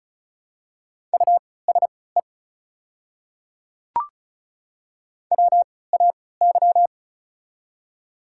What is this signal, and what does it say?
Morse code, heard as a keyed tone, about 36 wpm, spaced as for 20 wpm, 700 Hz. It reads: USE WAY
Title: 3x Character Spacing - Top 100 Words - 35wpm